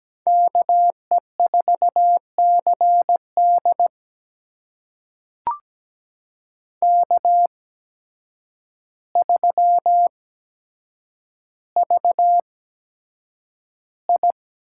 Morse code, heard as a keyed tone, about 17 wpm, spaced as for 8 wpm, 700 Hz. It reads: KE4CD K3VI
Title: USA Call Signs - 8x Character Spacing - 17wpm